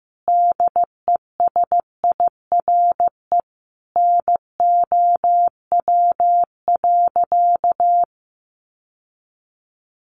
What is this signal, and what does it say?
Morse code, heard as a keyed tone, about 15 wpm, 700 Hz. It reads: DESIRE NOW.
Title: Sentences from Top 1100 Words 15wpm